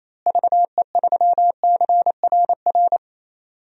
Morse code, heard as a keyed tone, about 28 wpm, 700 Hz. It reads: VE3CRR
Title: DXCC-100 Call Signs - 6 Characters - Repeated 3x - 28wpm